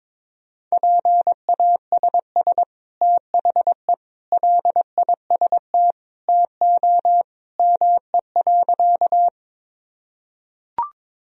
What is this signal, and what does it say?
Morse code, heard as a keyed tone, about 22 wpm, 700 Hz. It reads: PASS THE LIST TO ME.
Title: Sentences from Top 1000 Words - Review Part 2 - 22wpm